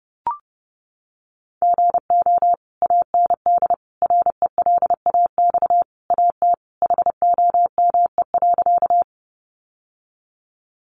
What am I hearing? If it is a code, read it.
Morse code, heard as a keyed tone, about 30 wpm, 700 Hz. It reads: GO AND RELAX AT HOME.
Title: Sentences from Top 1400 Words 30wpm